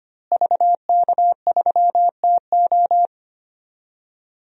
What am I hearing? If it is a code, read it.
Morse code, heard as a keyed tone, about 25 wpm, 700 Hz. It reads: VK3TO
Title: DXCC-100 Call Signs - 5 Characters - Repeated 3x - 25wpm